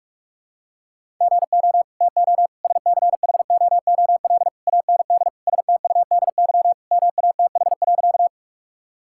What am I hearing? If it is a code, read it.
Morse code, heard as a keyed tone, about 46 wpm, 700 Hz. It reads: GO TO SCHOOL AND STUDY MATH.
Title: Sentences from Top 1000 Words - Review Part 6 - 45wpm